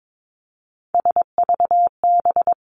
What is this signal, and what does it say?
Morse code, heard as a keyed tone, about 22 wpm, 700 Hz. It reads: SVB
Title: Three Challenging Letter-Number Combinations 22wpm